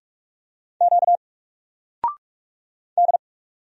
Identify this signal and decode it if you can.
Morse code, heard as a keyed tone, about 44 wpm, 700 Hz. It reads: Q D
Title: Single Letter-Number 45wpm